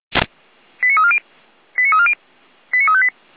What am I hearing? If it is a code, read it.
Five-tone selective call (ZVEI), heard as quick runs of five tones, tones of about 70 ms. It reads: 89340 89340 89358
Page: SDR/ZVEI